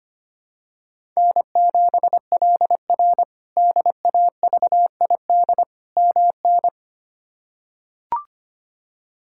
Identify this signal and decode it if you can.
Morse code, heard as a keyed tone, about 25 wpm, 700 Hz. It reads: N7LR DAVID MN
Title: K1USN SST QSO Practice - 25wpm